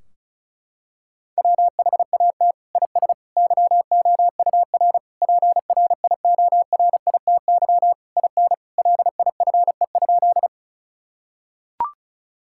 Morse code, heard as a keyed tone, about 36 wpm, 700 Hz. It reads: WHAT IS YOUR PRIORITY IN LIFE?
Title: Sentences from Top 1600 Words 35wpm